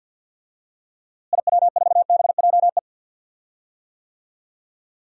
Morse code, heard as a keyed tone, about 52 wpm, 700 Hz. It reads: IW4BJE
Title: DXCC-100 Call Signs - 6 Characters - Repeated 2x - 50wpm